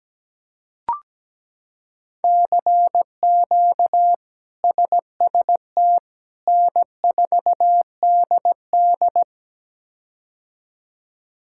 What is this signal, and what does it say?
Morse code, heard as a keyed tone, about 17 wpm, 700 Hz. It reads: CQ SST N4DD
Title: K1USN SST QSO Practice - 17wpm